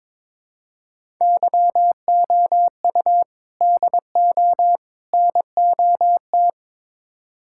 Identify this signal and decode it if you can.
Morse code, heard as a keyed tone, about 22 wpm, 700 Hz. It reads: YOU DO NOT
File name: practice / 22wpm / Build A Phrase - 5-Word Phrases - 22wpm